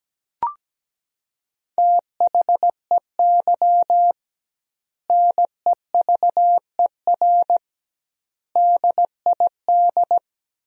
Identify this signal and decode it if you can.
Morse code, heard as a keyed tone, about 17 wpm, 700 Hz. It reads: THEY NEVER DID
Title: 2x Word-Spacing - Sentences from Top 300 Words - 17wpm